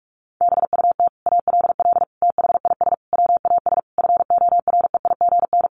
Morse code, heard as a keyed tone, about 44 wpm, 700 Hz. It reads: BUT ALL THIS WAS FOREIGN
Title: Sentences from Top 1400 Words 45wpm